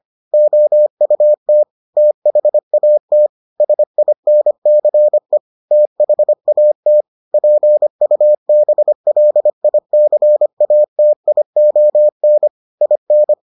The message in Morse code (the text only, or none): OUT THAT SINCE THAT PUBLICATION IN